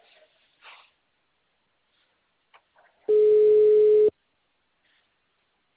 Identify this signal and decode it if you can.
Ringback tone (Europe, 425 Hz).
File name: out- -2016- - - .76722